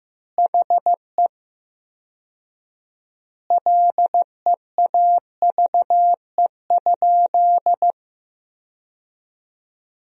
Morse code, heard as a keyed tone, about 15 wpm, 700 Hz. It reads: HE LEAVE?
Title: 4x Word-Spacing - Sentences from Top 400 Words - 15wpm